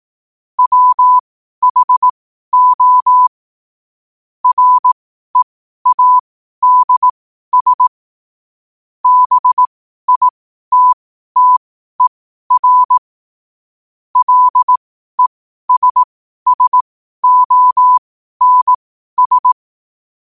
Morse code, heard as a keyed tone, about 18 wpm, spaced as for 12 wpm, 1000 Hz. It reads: WHO READS BITTER LESSONS
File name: qotdfiles / 12wpm